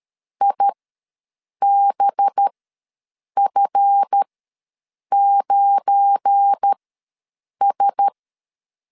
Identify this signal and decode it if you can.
Morse code, heard as a keyed tone, about 13 wpm, spaced as for 7 wpm, 800 Hz. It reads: IBF9S
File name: LEZIONI / 8 YJ45 / 8 YJ ws 12wpm 3342